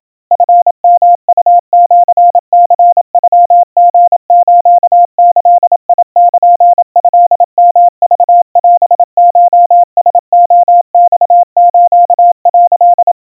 Morse code (Japanese Wabun code), heard as a keyed tone, about 27 wpm, 700 Hz. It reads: チヨウシニノリスギルトヨクオコラレマス」
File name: kawa / Morse sample / kamyu 100 cpm 700 Hz Crispy